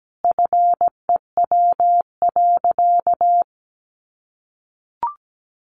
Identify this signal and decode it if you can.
Morse code, heard as a keyed tone, about 17 wpm, 700 Hz. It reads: FEW.
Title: Sentences from Top 1500 Words 17wpm